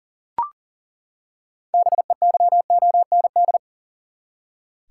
Morse code, heard as a keyed tone, about 39 wpm, 700 Hz. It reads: BEYOND
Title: Six Letter Words - 40wpm